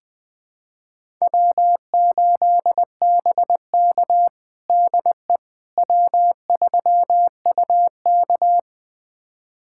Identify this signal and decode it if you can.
Morse code, heard as a keyed tone, about 20 wpm, 700 Hz. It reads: W8BK DE W3UK
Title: USA Callsign DE Callsign - 4 Characters - 20wpm